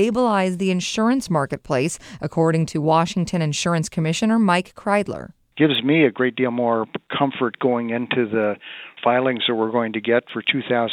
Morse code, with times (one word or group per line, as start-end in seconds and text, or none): none